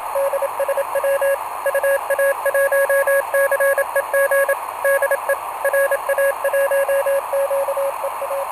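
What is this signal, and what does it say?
Morse code, heard as a keyed tone, about 27 wpm, 550 Hz, with radio noise. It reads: DSW UA1CEG DE RA1QEA
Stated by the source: Portable QRP contact on a home-made 3 watt transmitter with a long dipole, received by a station running home-made 1.2 watts, with key clicks and hard keying in question.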